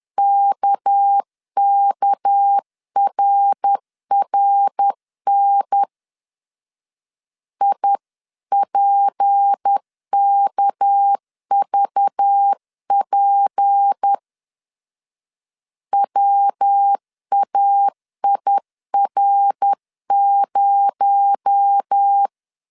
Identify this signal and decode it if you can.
Morse code, heard as a keyed tone, about 10 wpm, 800 Hz. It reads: KKRRN IPKVP WAIR0
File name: CW / DET / 1 6 1500C S C 10wpm 4252